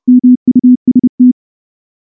Morse code, heard as a keyed tone, about 29 wpm, 250 Hz. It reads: MUST
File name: morse / morse-training-30WPM